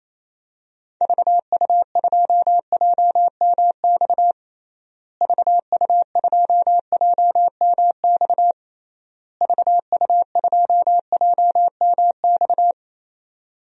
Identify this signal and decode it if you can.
Morse code, heard as a keyed tone, about 28 wpm, 700 Hz. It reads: VU2JMX VU2JMX VU2JMX
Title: DXCC-100 Call Signs - 6 Characters - Repeated 3x - 28wpm